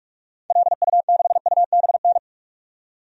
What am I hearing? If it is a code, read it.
Morse code, heard as a keyed tone, about 43 wpm, 700 Hz. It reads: LU6UBN
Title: DXCC-100 Call Signs - 6 Characters - 45wpm